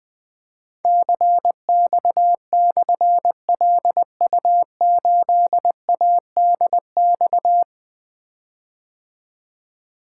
Morse code, heard as a keyed tone, about 20 wpm, 700 Hz. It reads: CX/LU8ADX
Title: Difficult Call Signs 20wpm